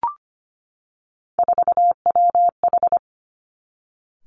Morse code, heard as a keyed tone, about 25 wpm, 700 Hz. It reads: 4WH
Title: Three Challenging Letter-Number Combinations 25wpm